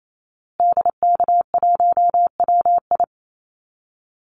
Morse code, heard as a keyed tone, about 28 wpm, 700 Hz. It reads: DK1WI
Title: Call Signs Repeated 2x - 28wpm